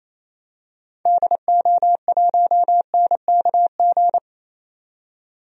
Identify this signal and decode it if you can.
Morse code, heard as a keyed tone, about 28 wpm, 700 Hz. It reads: DO1NKG
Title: DXCC-100 Call Signs - 6 Characters - 28wpm